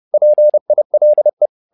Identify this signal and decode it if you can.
Morse code, heard as a keyed tone, about 30 wpm, 600 Hz. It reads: PILE
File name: learning center / tower trouble / tower trouble 30wpm